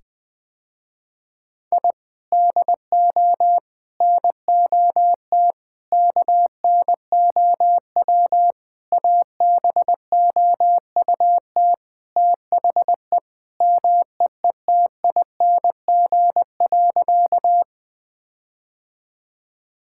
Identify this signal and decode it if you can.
Morse code, heard as a keyed tone, about 19 wpm, 700 Hz. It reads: I DO NOT KNOW ABOUT THE MEETING.
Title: Sentences with 5-Word Phrases - Spoken Beginning - 20wpm